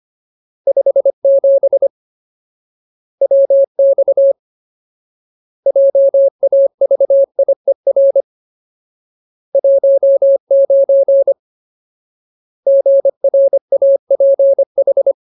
Morse code, heard as a keyed tone, about 25 wpm, 550 Hz. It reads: 57 WX JAVIER 19 GRAPH
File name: Ultimate Word List / 25WPM Ultimate